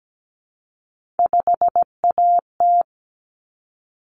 Morse code, heard as a keyed tone, about 17 wpm, 700 Hz. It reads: 5AT
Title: Common Cut Numbers - 3-Digit 17wpm